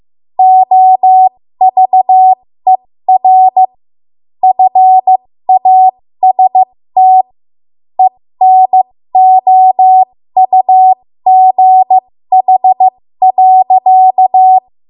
Morse code, heard as a keyed tone, about 15 wpm, spaced as for 13 wpm, 750 Hz. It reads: OVER FAST ENOUGH.
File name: Morse / 13WPM